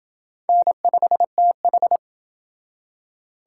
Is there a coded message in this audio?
Yes: Morse code from 0.5 to 2.0 s, heard as a keyed tone, about 27 wpm, 700 Hz. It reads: N5TH